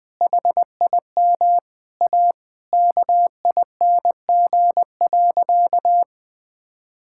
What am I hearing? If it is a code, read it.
Morse code, heard as a keyed tone, about 20 wpm, 700 Hz. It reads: HIM A KING.
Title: Sentences from Top 1000 Words - Review Part 5 - 20wpm